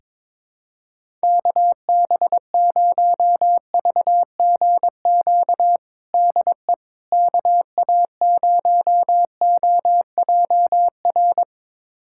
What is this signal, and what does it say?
Morse code, heard as a keyed tone, about 21 wpm, 700 Hz. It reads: KB0VGQ DE KA0OJR
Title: USA Callsign DE Callsign 22wpm